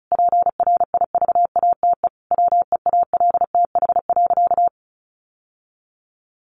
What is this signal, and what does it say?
Morse code, heard as a keyed tone, about 34 wpm, 700 Hz. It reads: PRIVATE WEALTH.
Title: Sentences from Top 1200 Words 35wpm